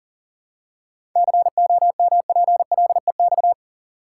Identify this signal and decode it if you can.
Morse code, heard as a keyed tone, about 40 wpm, 700 Hz. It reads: COMPLEX
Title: Seven Letter Words - 40wpm